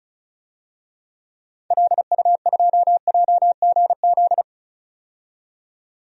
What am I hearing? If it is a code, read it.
Morse code, heard as a keyed tone, about 36 wpm, 700 Hz. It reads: LU2JGZ
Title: DXCC-100 Call Signs - 6 Characters - 35wpm